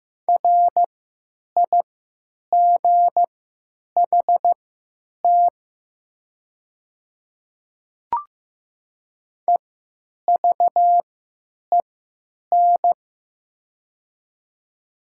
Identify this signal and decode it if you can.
Morse code, heard as a keyed tone, about 15 wpm, spaced as for 9 wpm, 700 Hz. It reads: RIGHT EVEN
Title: 3x Character Spacing - Top 200 Words - 15wpm